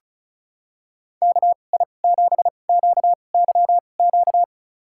Morse code, heard as a keyed tone, about 36 wpm, spaced as for 25 wpm, 700 Hz. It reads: KI7QYQ
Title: USA Call Signs - 6 Characters - 2x Character Spacing - 35wpm